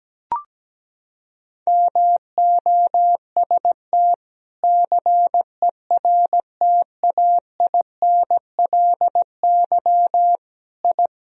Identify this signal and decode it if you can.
Morse code, heard as a keyed tone, about 17 wpm, 700 Hz. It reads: MOST CERTAINLY I
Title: Sentences from Top 1000 Words - Review Part 1 - 17wpm